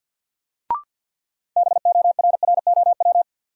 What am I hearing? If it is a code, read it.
Morse code, heard as a keyed tone, about 50 wpm, 700 Hz. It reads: BORROW
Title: Sentences from Top 1500 Words - New Word Drill Alt - 50wpm